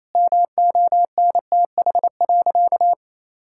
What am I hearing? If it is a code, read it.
Morse code, heard as a keyed tone, about 28 wpm, 700 Hz. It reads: MONTH.
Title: Sentences from Top 1000 Words - Review Part 5 - 28wpm